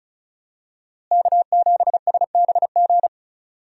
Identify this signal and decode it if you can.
Morse code, heard as a keyed tone, about 36 wpm, 700 Hz. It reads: K7SBG
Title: USA Call Signs - 5 Characters - Repeated 2x - 35wpm